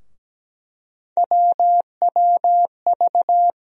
Morse code, heard as a keyed tone, about 17 wpm, 700 Hz. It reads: WWV
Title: USA Call Sign Postfixes - 3 Letters 17wpm